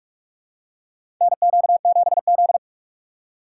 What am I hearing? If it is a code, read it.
Morse code, heard as a keyed tone, about 46 wpm, 700 Hz. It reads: NQ7Z
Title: USA Call Signs - 2x Character Spacing - 45wpm